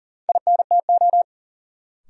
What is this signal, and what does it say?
Morse code, heard as a keyed tone, about 40 wpm, 700 Hz. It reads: INTO
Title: Sentences from Top 600 Words - New Word Drill - 40wpm